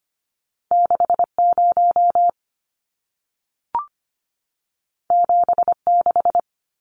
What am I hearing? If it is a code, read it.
Morse code, heard as a keyed tone, about 25 wpm, 700 Hz. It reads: 60 76
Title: Numbers - 2-Digit 25wpm